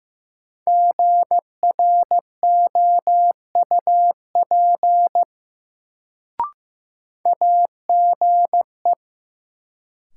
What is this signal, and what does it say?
Morse code, heard as a keyed tone, about 15 wpm, 700 Hz. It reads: GROUP AGE
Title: Sets of 3 Words - New Word Drill - 15wpm